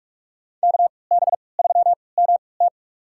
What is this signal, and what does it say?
Morse code, heard as a keyed tone, about 42 wpm, spaced as for 25 wpm, 700 Hz. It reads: KB3MT
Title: USA Call Signs - 5 Characters - 3x Character Spacing - 45wpm